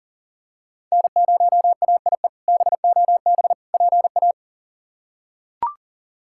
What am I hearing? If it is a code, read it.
Morse code, heard as a keyed tone, about 38 wpm, 700 Hz. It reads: N0AIE BOB PA
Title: K1USN SST QSO Practice - 40wpm